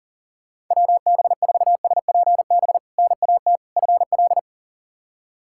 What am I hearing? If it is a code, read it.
Morse code, heard as a keyed tone, about 40 wpm, 700 Hz. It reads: WB4SPB NAT FL
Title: K1USN SST QSO Practice - 40wpm